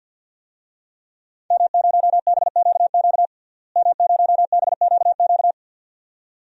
Morse code, heard as a keyed tone, about 52 wpm, 700 Hz. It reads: M0BQQ M0BQQ
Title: DXCC-100 Call Signs - 5 Characters - Repeated 2x - 50wpm